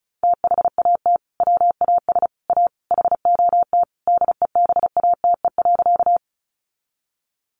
Morse code, heard as a keyed tone, about 34 wpm, 700 Hz. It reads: THAT WAS A HOT DEBATE.
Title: Sentences from Top 1100 Words 35wpm